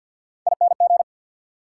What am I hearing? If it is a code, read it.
Morse code, heard as a keyed tone, about 54 wpm, 690 Hz. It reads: ING
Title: Prefixes and Suffixes 50wpm